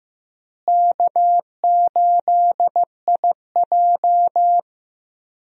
Morse code, heard as a keyed tone, about 15 wpm, 700 Hz. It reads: K8IJ
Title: USA Call Signs - 4 Characters - Repeated 3x - 15wpm